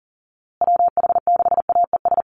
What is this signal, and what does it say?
Morse code, heard as a keyed tone, about 40 wpm, 700 Hz. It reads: WH6AES
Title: USA Call Signs Repeated 3x 40wpm